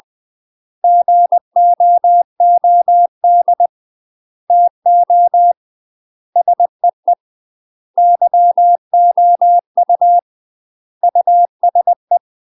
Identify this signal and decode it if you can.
Morse code, heard as a keyed tone, about 20 wpm, 700 Hz. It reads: GOOD TO SEE YOU USE
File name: UK MASTER QSOs TRAINING/20WPM